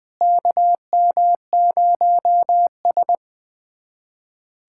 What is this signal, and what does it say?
Morse code, heard as a keyed tone, about 20 wpm, 700 Hz. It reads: KM0S
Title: USA Callsign DE Callsign - 4 Characters - 20wpm